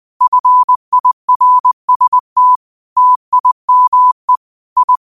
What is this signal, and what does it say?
Morse code, heard as a keyed tone, about 20 wpm, 1000 Hz. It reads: FIRST TIME I
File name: qotdfiles / 20wpm